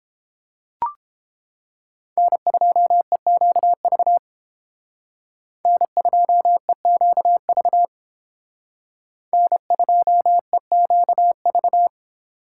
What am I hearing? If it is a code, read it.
Morse code, heard as a keyed tone, about 30 wpm, 700 Hz. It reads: N2EQV N2EQV N2EQV